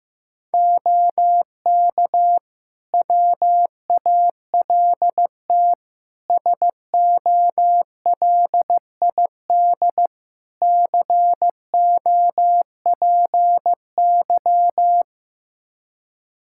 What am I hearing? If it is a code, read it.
Morse code, heard as a keyed tone, about 15 wpm, 700 Hz. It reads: OK WALT SOLID COPY
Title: Practice QSO - Exchange 3 - More Info - 15wpm